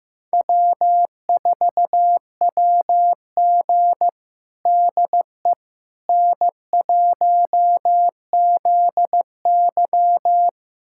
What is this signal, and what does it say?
Morse code, heard as a keyed tone, about 15 wpm, 700 Hz. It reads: W4WG DE N1ZY
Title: USA Callsign DE Callsign - 4 Characters - 15wpm